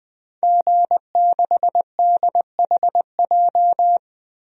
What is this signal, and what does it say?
Morse code, heard as a keyed tone, about 20 wpm, 700 Hz. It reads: G6DHJ
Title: DXCC-100 Call Signs - 5 Characters - Repeated 3x - 20wpm